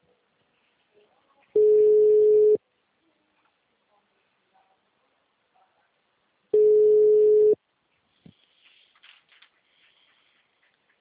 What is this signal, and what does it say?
Ringback tone (Europe, 425 Hz), with speech.